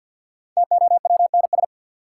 Morse code, heard as a keyed tone, about 51 wpm, 700 Hz. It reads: TOWNS
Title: Sentences from Top 1000 Words - Review Part 6 - New Word Drill - 50wpm